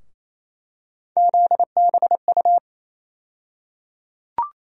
Morse code, heard as a keyed tone, about 28 wpm, 700 Hz. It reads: ZBU
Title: USA Call Sign Postfixes - 3 Letters 28wpm